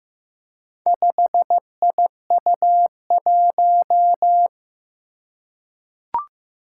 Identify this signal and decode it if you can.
Morse code, heard as a keyed tone, about 15 wpm, 700 Hz. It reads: 5IU1
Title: Four Challenging Letter-Number Combinations 15wpm